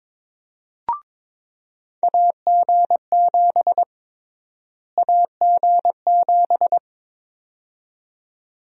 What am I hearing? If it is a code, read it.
Morse code, heard as a keyed tone, about 22 wpm, 700 Hz. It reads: AG7 AG7
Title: USA Call Sign Prefixes - Repeated 2x - 22wpm